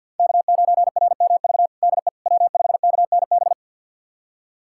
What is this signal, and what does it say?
Morse code, heard as a keyed tone, about 51 wpm, 700 Hz. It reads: K9RMV DE W5KNB
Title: USA Callsign DE Callsign - 5 Characters - 50wpm